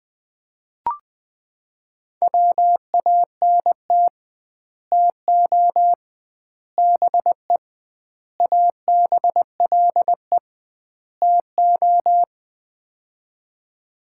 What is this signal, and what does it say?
Morse code, heard as a keyed tone, about 20 wpm, 700 Hz. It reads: WANT TO BE ABLE TO
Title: 2x Word-Spacing - Sets of 5 Words - 20wpm